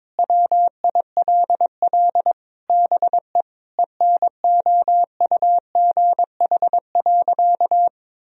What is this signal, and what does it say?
Morse code, heard as a keyed tone, about 22 wpm, 700 Hz. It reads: WILL BE ENOUGH.
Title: Sentences from Top 1000 Words - Review Part 2 - 22wpm